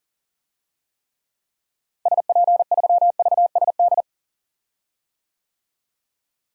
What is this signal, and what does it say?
Morse code, heard as a keyed tone, about 40 wpm, 700 Hz. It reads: SP3VSD
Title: DXCC-100 Call Signs - 6 Characters - Repeated 3x - 40wpm